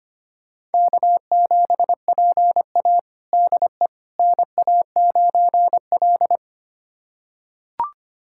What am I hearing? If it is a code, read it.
Morse code, heard as a keyed tone, about 25 wpm, 700 Hz. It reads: K7PA DE NA9L
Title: USA Callsign DE Callsign - 4 Characters - 25wpm